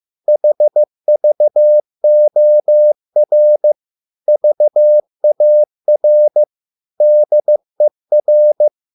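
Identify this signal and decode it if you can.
Morse code, heard as a keyed tone, about 15 wpm, 600 Hz. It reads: HVOR VAR DER
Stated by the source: Morse code at a slow 15 wpm from CW med Gnister.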